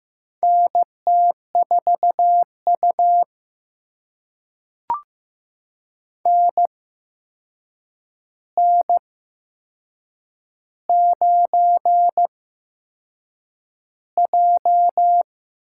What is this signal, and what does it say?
Morse code, heard as a keyed tone, about 15 wpm, spaced as for 6 wpm, 700 Hz. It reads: NT4U NN9J
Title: USA Call Signs - 4 Characters - 8x Character Spacing - 15wpm